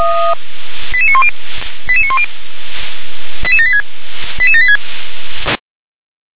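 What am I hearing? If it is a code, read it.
Five-tone selective call (ZVEI), heard as quick runs of five tones, tones of about 70 ms. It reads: 88910 88910 88786 88786
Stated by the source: Five-tone ZVEI selective call received on SDR.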